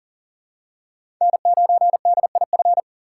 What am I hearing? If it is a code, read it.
Morse code, heard as a keyed tone, about 40 wpm, 700 Hz. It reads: N9DIF
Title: USA Call Signs - 5 Characters - 6x Character Spacing - 40wpm